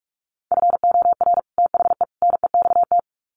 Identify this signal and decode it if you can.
Morse code, heard as a keyed tone, about 45 wpm, 700 Hz. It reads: FOR THE NEXT
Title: Sets of 3 Words 45wpm